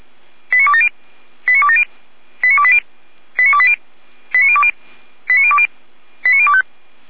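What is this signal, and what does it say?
Five-tone selective call (ZVEI), heard as quick runs of five tones, tones of about 70 ms. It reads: 89270 89270 89280 89280 89020 89020 89025
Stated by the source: Received on SDR; ZVEI five-tone call 89208, channel 455.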